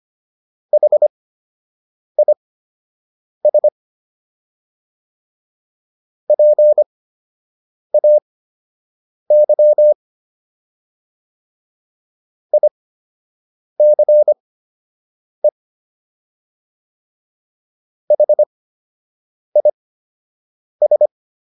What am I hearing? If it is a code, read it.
Morse code, heard as a keyed tone, about 25 wpm, spaced as for 7 wpm, 600 Hz. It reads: HIS PAY ICE HIS